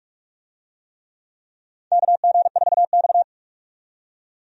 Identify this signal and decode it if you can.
Morse code, heard as a keyed tone, about 46 wpm, 700 Hz. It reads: KG4X
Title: USA Call Signs - 4 Characters - 45wpm